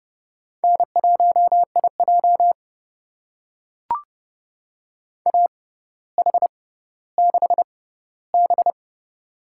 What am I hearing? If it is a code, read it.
Morse code, heard as a keyed tone, about 30 wpm, spaced as for 15 wpm, 700 Hz. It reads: N1IJ AH6B
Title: USA Call Signs - 4 Characters - 6x Character Spacing - 30wpm